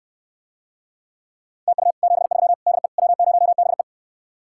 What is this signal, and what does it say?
Morse code, heard as a keyed tone, about 69 wpm, 700 Hz. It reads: TU 73 DE W0BE